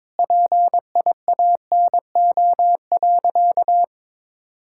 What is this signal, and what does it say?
Morse code, heard as a keyed tone, about 22 wpm, 700 Hz. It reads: PIANO.
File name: practice / 22wpm / Easy Sentences with Hints - 22wpm